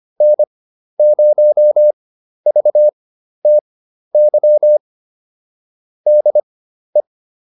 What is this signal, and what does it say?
Morse code, heard as a keyed tone, about 25 wpm, spaced as for 12 wpm, 600 Hz. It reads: N0VTY DE